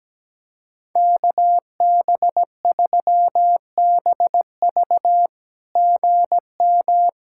Morse code, heard as a keyed tone, about 17 wpm, 700 Hz. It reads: KB3BV GM
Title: SOTA QSO Practice 17wpm